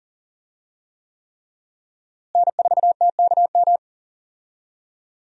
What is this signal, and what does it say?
Morse code, heard as a keyed tone, about 40 wpm, 700 Hz. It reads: N4TKM
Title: USA Call Signs - 5 Characters - Repeated 3x - 40wpm